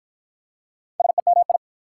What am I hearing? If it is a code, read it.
Morse code, heard as a keyed tone, about 48 wpm, 690 Hz. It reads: SEMI